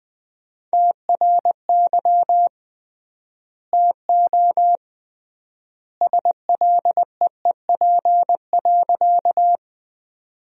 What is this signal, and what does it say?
Morse code, heard as a keyed tone, about 20 wpm, 700 Hz. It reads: TRY TO SLEEP.
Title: 3x Word-Spacing - Sentences from Top 500 Words - 20wpm